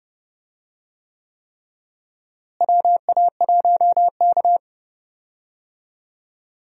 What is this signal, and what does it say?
Morse code, heard as a keyed tone, about 30 wpm, 700 Hz. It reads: WA1K
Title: USA Call Signs - 4 Characters - Repeated 3x - 30wpm